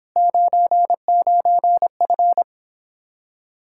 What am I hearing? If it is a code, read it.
Morse code, heard as a keyed tone, about 29 wpm, 700 Hz. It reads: 99F